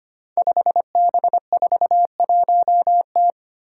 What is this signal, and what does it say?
Morse code, heard as a keyed tone, about 25 wpm, 700 Hz. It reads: 5B41T
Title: Five Challenging Letter-Number Combinations 25wpm